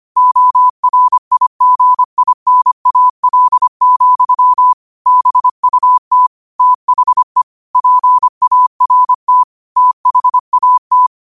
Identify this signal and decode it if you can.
Morse code, heard as a keyed tone, about 25 wpm, 1000 Hz. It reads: ORIGINAL, BUT THE PART THAT
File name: 25wpm